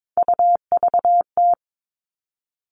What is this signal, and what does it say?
Morse code, heard as a keyed tone, about 22 wpm, 700 Hz. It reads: UVT
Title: All Cut Numbers - 3-Digit 22wpm